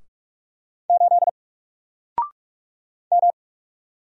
Morse code, heard as a keyed tone, about 44 wpm, 700 Hz. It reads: 8 M
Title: Single Letter-Number 45wpm